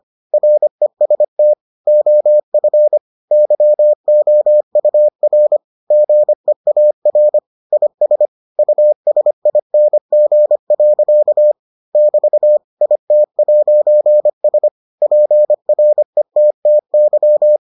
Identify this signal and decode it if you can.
Morse code, heard as a keyed tone, about 25 wpm, 600 Hz. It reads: REST OF YOUR GEAR IS USING. <BT> IT'S PRETTY